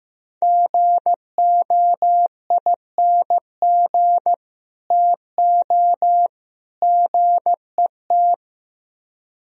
Morse code, heard as a keyed tone, about 16 wpm, 700 Hz. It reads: GOING TO GET
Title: Speed-Racing - Sets of 3 Words 15wpm